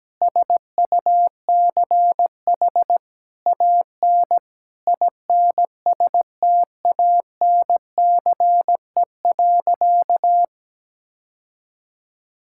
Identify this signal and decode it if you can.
Morse code, heard as a keyed tone, about 17 wpm, 700 Hz. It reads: SUCH AN INSTANCE.
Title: Sentences from Top 1000 Words - Review Part 5 - 17wpm